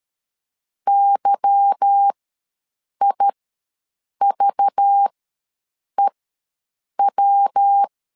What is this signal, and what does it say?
Morse code, heard as a keyed tone, about 13 wpm, spaced as for 7 wpm, 800 Hz. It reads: YIVEW
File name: LEZIONI / 10 SIGNS / 10A SLASH ws 12wpm 3443